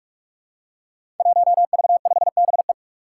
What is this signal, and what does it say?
Morse code, heard as a keyed tone, about 46 wpm, 700 Hz. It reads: 1V5BE
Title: Five Challenging Letter-Number Combinations 45wpm